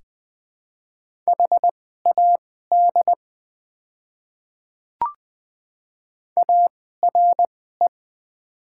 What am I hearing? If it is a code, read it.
Morse code, heard as a keyed tone, about 20 wpm, spaced as for 15 wpm, 700 Hz. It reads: HAD ARE
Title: 2x Character Spacing - Top 100 Words - 20wpm